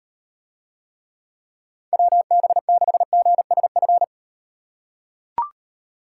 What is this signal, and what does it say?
Morse code, heard as a keyed tone, about 36 wpm, 700 Hz. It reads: WB6GSF